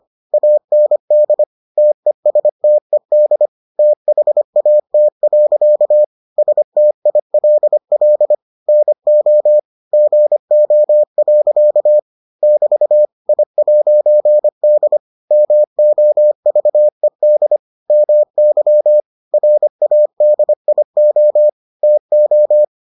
Morse code, heard as a keyed tone, about 25 wpm, 600 Hz. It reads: AND TESTED THAT. STILL NO GO. <BT> I'D MOVED MY RADIO TO